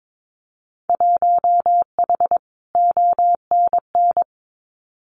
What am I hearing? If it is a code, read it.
Morse code, heard as a keyed tone, about 22 wpm, 700 Hz. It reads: 1H ONN
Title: Winter Field Day Exchanges 22wpm